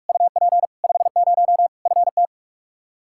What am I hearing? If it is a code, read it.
Morse code, heard as a keyed tone, about 46 wpm, 700 Hz. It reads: UP 50 FT
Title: Practice QSO - Exchange 2 - Info - 45wpm